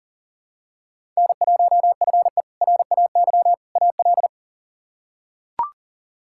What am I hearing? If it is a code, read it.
Morse code, heard as a keyed tone, about 40 wpm, 700 Hz. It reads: N1FE RAY AL
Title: K1USN SST QSO Practice - 40wpm